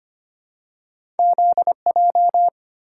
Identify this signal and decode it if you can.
Morse code, heard as a keyed tone, about 25 wpm, 700 Hz. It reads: ZJ
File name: Two Letter-Number Combinations - 25wpm